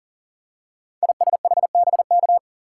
Morse code, heard as a keyed tone, about 38 wpm, 700 Hz. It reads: ISHBK
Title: Code Groups - 5 Letters - 40wpm